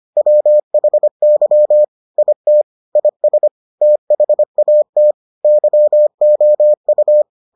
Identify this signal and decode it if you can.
Morse code, heard as a keyed tone, about 25 wpm, 600 Hz. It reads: WHY IT IS THAT YOU